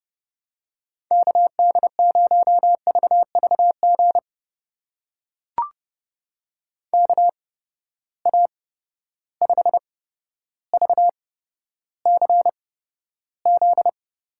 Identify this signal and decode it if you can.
Morse code, heard as a keyed tone, about 31 wpm, spaced as for 13 wpm, 700 Hz. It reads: KD0VVG KA5VCZ
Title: USA Call Signs - 6 Characters - 8x Character Spacing - 30wpm